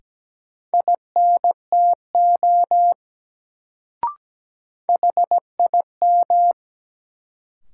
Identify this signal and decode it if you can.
Morse code, heard as a keyed tone, about 17 wpm, 700 Hz. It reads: INTO HIM
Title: Sets of 2 Words - New Word Drill - 17wpm